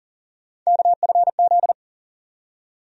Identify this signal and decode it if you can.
Morse code, heard as a keyed tone, about 40 wpm, 700 Hz. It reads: KFZ
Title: USA Call Sign Postfixes - 3 Letters 40wpm